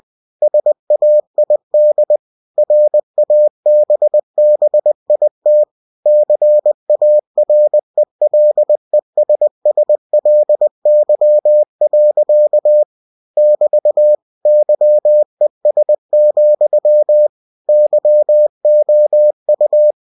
Morse code, heard as a keyed tone, about 20 wpm, 600 Hz. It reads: SAID RABBIT CARELESSLY. <BT> YES, YOU
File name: cw-books / winnie-the-pooh / winnie-20wpm-page-0061